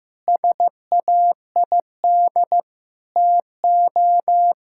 Morse code, heard as a keyed tone, about 16 wpm, 700 Hz. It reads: SAID TO
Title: Sentences from Top 1600 Words 15wpm